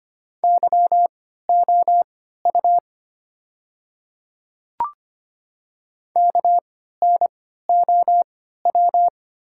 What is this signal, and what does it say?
Morse code, heard as a keyed tone, about 25 wpm, spaced as for 14 wpm, 700 Hz. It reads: YOU KNOW
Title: 3x Character Spacing - Top 100 Words - 25wpm